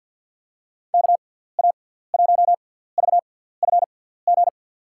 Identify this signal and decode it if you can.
Morse code, heard as a keyed tone, about 52 wpm, spaced as for 17 wpm, 700 Hz. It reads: KA1VFG
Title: USA Call Signs - 6 Characters - 6x Character Spacing - 50wpm